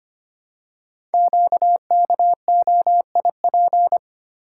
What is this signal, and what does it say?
Morse code, heard as a keyed tone, about 25 wpm, 700 Hz. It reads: QKOIP
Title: Code Groups - 5 Letters - 25wpm